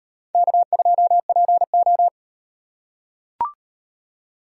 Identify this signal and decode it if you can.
Morse code, heard as a keyed tone, about 38 wpm, 700 Hz. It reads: K2PO